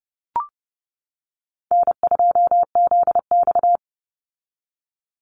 Morse code, heard as a keyed tone, about 30 wpm, 700 Hz. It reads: N2ZX
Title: Call Signs - 30wpm